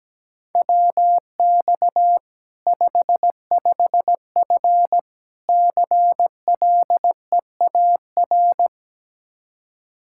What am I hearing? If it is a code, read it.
Morse code, heard as a keyed tone, about 17 wpm, 700 Hz. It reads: WX 55F CLEAR
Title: Practice QSO - Exchange 2 - Info - 17wpm